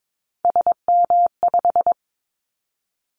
Morse code, heard as a keyed tone, about 22 wpm, 700 Hz. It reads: SM5
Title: Callsign Prefixes for Top 25 DXCC - 22wpm